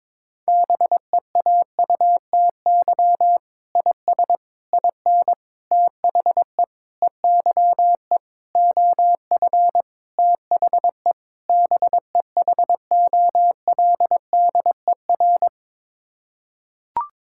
Morse code, heard as a keyed tone, about 22 wpm, 700 Hz. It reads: BEAUTY IS IN THE EYE OF THE BEHOLDER
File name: practice / 22wpm / Sets of 8 Words - 22wpm